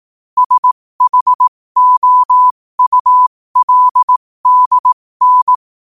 Morse code, heard as a keyed tone, about 18 wpm, spaced as for 15 wpm, 1000 Hz. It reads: SHOULDN
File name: qotdfiles / 15wpm